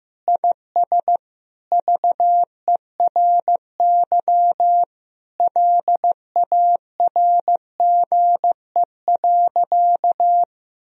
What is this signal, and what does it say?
Morse code, heard as a keyed tone, about 15 wpm, 700 Hz. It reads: IS VERY LARGE.
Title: Sentences from Top 1000 Words - Review Part 1 - 15wpm